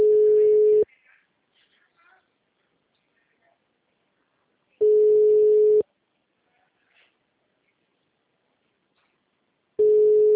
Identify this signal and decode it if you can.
Ringback tone (Europe, 425 Hz), with speech.
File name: out- -2027- - - .23055